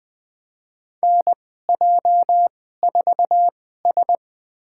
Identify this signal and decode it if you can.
Morse code, heard as a keyed tone, about 20 wpm, spaced as for 14 wpm, 700 Hz. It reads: NJ4S
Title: USA Call Signs - 2x Character Spacing - 20wpm